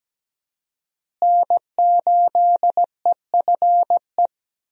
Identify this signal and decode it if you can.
Morse code, heard as a keyed tone, about 17 wpm, 700 Hz. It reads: N8EFE
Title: USA Call Signs - 5 Characters - Repeated 3x - 17wpm